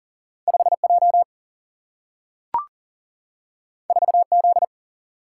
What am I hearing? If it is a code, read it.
Morse code, heard as a keyed tone, about 40 wpm, 700 Hz. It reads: 5 J 4 Z
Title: Single-Letter Single-Number 40wpm